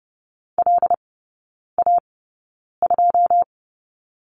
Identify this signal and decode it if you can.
Morse code, heard as a keyed tone, about 30 wpm, spaced as for 9 wpm, 700 Hz. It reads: LA2
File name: 081 Lesson 14e - Number 2 - Three Characters 30wpm